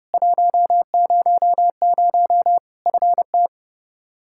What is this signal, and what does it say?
Morse code, heard as a keyed tone, about 30 wpm, 700 Hz. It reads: 100 FT
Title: Practice QSO - Exchange 2 - Info - 30wpm